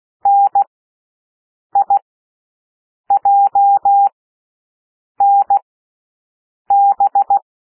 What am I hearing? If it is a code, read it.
Morse code, heard as a keyed tone, about 16 wpm, spaced as for 6 wpm, 800 Hz. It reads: NIJNB
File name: cwmp3 / L1 SOUND LETTERS 16 wpm